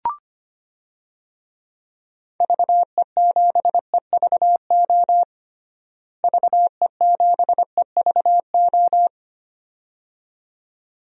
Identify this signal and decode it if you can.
Morse code, heard as a keyed tone, about 25 wpm, 700 Hz. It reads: VE7EVO VE7EVO
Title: DXCC-100 Call Signs - 6 Characters - Repeated 2x - 25wpm